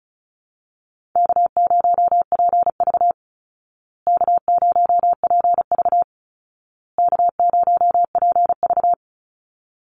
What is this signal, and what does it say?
Morse code, heard as a keyed tone, about 34 wpm, 700 Hz. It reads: K0PV K0PV K0PV
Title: Easy Call Signs Repeated 3x - 35wpm